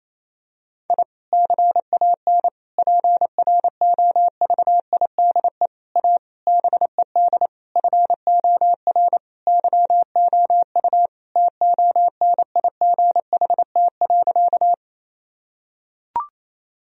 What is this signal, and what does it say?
Morse code, heard as a keyed tone, about 28 wpm, 700 Hz. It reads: I CAN PROVIDE A BED FOR YOU TONIGHT.
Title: Sentences from Top 1000 Words - Review Part 3 - 28wpm